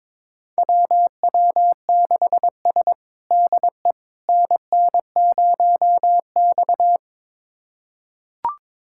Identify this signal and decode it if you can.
Morse code, heard as a keyed tone, about 22 wpm, 700 Hz. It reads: WW6S DE NN0X
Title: USA Callsign DE Callsign - 4 Characters - 22wpm